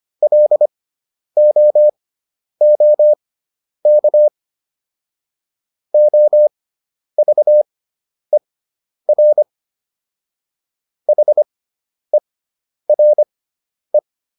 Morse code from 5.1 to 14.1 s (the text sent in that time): OVER HERE